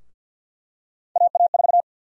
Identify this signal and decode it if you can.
Morse code, heard as a keyed tone, about 48 wpm, 700 Hz. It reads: AA4
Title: USA Call Sign Prefixes - Repeated 2x - 50wpm